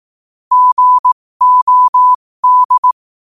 Morse code, heard as a keyed tone, about 18 wpm, spaced as for 15 wpm, 1000 Hz. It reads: GOD
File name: qotdfiles / 15wpm